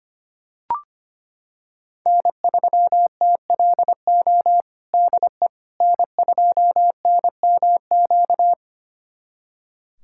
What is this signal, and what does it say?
Morse code, heard as a keyed tone, about 25 wpm, 700 Hz. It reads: N3TLO DE N2NMQ
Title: USA Callsign DE Callsign - 5 Characters - 25wpm